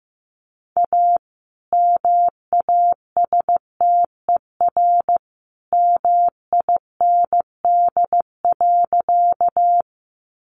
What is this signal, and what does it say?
Morse code, heard as a keyed tone, about 15 wpm, 700 Hz. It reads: A MASTER MIND.
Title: Sentences from Top 1300 Words 15wpm